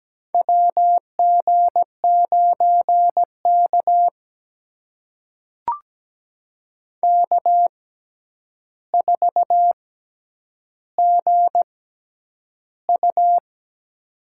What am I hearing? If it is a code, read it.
Morse code, heard as a keyed tone, about 17 wpm, spaced as for 9 wpm, 700 Hz. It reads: WG9K K4GU
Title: USA Call Signs - 4 Characters - 6x Character Spacing - 17wpm